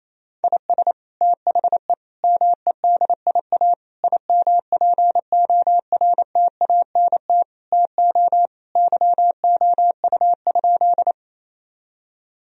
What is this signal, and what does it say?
Morse code, heard as a keyed tone, about 28 wpm, 700 Hz. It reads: IS THE MEDIA IMPORTANT TO YOU?
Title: Sentences from Top 1000 Words - Review Part 4 - 28wpm